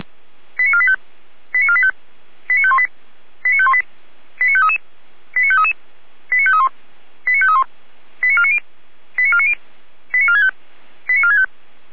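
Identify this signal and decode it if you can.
Five-tone selective call (ZVEI), heard as quick runs of five tones, tones of about 70 ms. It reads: 89485 89485 89518 89518 89533 89533 89531 89531 89490 89490 89465 89465
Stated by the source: ZVEI five-tone selective call received with an SDR.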